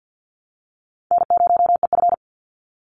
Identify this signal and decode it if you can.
Morse code, heard as a keyed tone, about 50 wpm, 700 Hz. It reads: N0EF
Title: Easy Call Signs - 50wpm